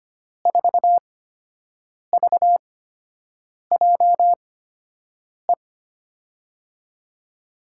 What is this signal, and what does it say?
Morse code, heard as a keyed tone, about 25 wpm, spaced as for 7 wpm, 700 Hz. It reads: 4VJE
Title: USA Call Signs - 6 Characters - 8x Character Spacing - 25wpm